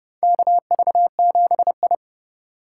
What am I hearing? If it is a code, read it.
Morse code, heard as a keyed tone, about 32 wpm, 700 Hz. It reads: KV7I
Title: USA Call Signs - 4 Characters - 2x Character Spacing - 30wpm